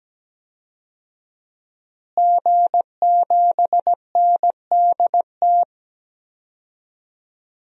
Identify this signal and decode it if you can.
Morse code, heard as a keyed tone, about 17 wpm, 700 Hz. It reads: G7NDT
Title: DXCC-100 Call Signs - 5 Characters - Repeated 2x - 17wpm